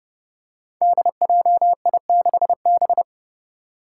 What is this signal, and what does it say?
Morse code, heard as a keyed tone, about 32 wpm, 700 Hz. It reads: DJI6B
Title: Five Challenging Letter-Number Combinations 30wpm